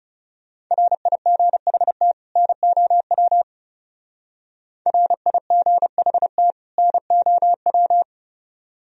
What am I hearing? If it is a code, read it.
Morse code, heard as a keyed tone, about 37 wpm, 700 Hz. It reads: RIGHT NOW RIGHT NOW
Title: Speed-Racing - Sets of 2 Words 30wpm